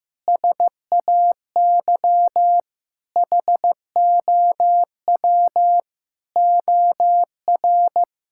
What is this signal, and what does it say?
Morse code, heard as a keyed tone, about 14 wpm, 700 Hz. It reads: SAY HOW OR
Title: Sentences from Top 1000 Words - Review Part 6 - 15wpm